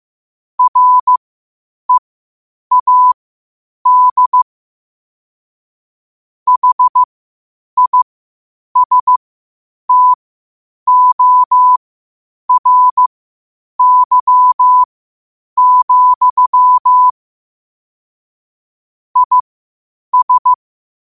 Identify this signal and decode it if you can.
Morse code, heard as a keyed tone, about 15 wpm, spaced as for 8 wpm, 1000 Hz. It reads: READ HISTORY, IS
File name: qotdfiles / 8wpm